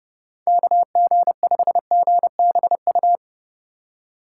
Morse code, heard as a keyed tone, about 30 wpm, 700 Hz. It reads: KG5GBU
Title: USA Call Signs - 3x Character Spacing - 30wpm